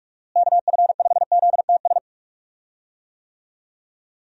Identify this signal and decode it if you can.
Morse code, heard as a keyed tone, about 46 wpm, 700 Hz. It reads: KF5ZTS
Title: USA Call Signs - 6 Characters - Repeated 3x - 45wpm